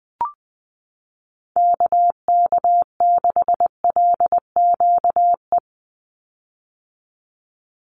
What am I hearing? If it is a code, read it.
Morse code, heard as a keyed tone, about 20 wpm, 700 Hz. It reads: KK6LQE
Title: USA Call Signs 20wpm